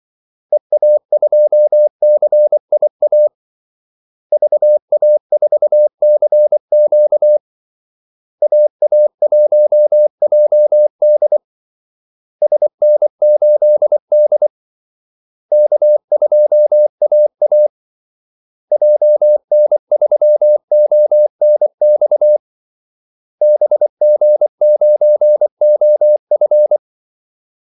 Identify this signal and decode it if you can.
Morse code, heard as a keyed tone, about 24 wpm, 600 Hz. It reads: EA2CIA VA4CQ AA1JD SN8D K2AA JN3ONX BG9OF
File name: Callsigns-24wpm